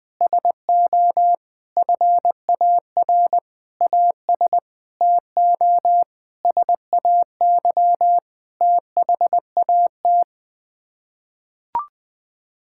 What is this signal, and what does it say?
Morse code, heard as a keyed tone, about 20 wpm, 700 Hz. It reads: SO FAR AS TO SAY THAT
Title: Sets of 9 Words - 20wpm